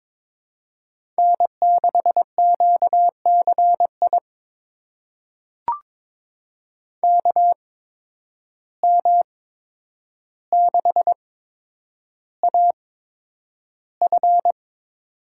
Morse code, heard as a keyed tone, about 22 wpm, spaced as for 9 wpm, 700 Hz. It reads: N6QCI KM6AF
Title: USA Call Signs - 5 Characters - 8x Character Spacing - 22wpm